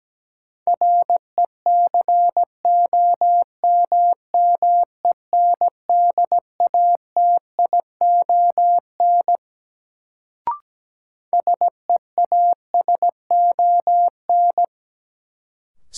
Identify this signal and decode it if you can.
Morse code, heard as a keyed tone, about 17 wpm, 700 Hz. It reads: RECOMMENDATION SEASON
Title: Sentences from Top 1400 Words - New Word Drill - 17wpm